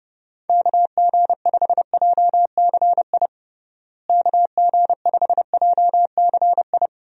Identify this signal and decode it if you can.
Morse code, heard as a keyed tone, about 32 wpm, 700 Hz. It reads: KG5JCI KG5JCI
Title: USA Call Signs - 6 Characters - Repeated 3x - 30wpm